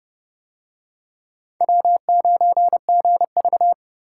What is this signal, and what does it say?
Morse code, heard as a keyed tone, about 29 wpm, 700 Hz. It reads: W9GV